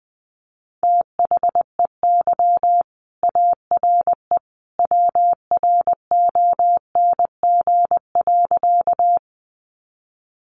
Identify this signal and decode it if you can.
Morse code, heard as a keyed tone, about 20 wpm, 700 Hz. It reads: THEY ARE WRONG.
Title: Sentences from Top 900 Words 20wpm